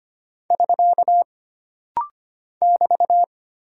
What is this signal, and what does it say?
Morse code, heard as a keyed tone, about 25 wpm, 700 Hz. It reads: <SK> <BT>
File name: Single Punctuation-Prosign - Rapid-Fire - 25wpm